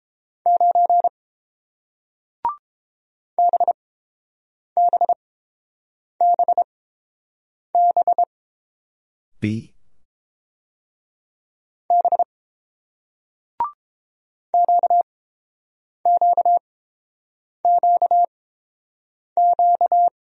Morse code, heard as a keyed tone, about 30 wpm, spaced as for 7 wpm, 700 Hz. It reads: 9 BBBB B QQQQ